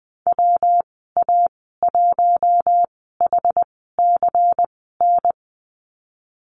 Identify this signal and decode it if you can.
Morse code, heard as a keyed tone, about 20 wpm, spaced as for 15 wpm, 700 Hz. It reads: WA1HCN
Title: 128 Lesson 21g - Number 6 - Call Signs 20wpm